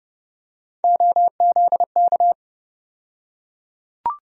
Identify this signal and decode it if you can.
Morse code, heard as a keyed tone, about 30 wpm, 700 Hz. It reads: OZK
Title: USA Call Sign Postfixes - 3 Letters 30wpm